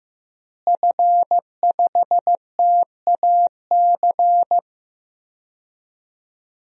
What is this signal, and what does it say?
Morse code, heard as a keyed tone, about 16 wpm, 700 Hz. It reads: F5TAC
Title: DXCC-100 Call Signs - 5 Characters - Repeated 2x - 15wpm